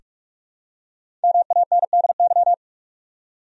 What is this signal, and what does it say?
Morse code, heard as a keyed tone, about 43 wpm, 700 Hz. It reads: MANDY
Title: USA Common Names - Max 5 Letters - 45wpm